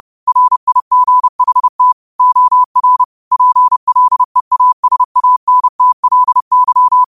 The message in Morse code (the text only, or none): RIGHT OR PLEASANTLY